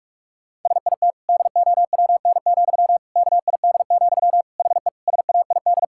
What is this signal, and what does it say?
Morse code, heard as a keyed tone, about 41 wpm, 700 Hz. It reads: SIT DOWN, KID, HE SAID
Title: Sentences from Top 1600 Words 45wpm